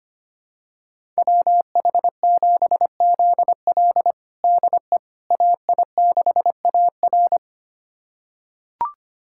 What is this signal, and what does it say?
Morse code, heard as a keyed tone, about 25 wpm, 700 Hz. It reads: WH7ZL DE AI6AR
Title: USA Callsign DE Callsign - 5 Characters - 25wpm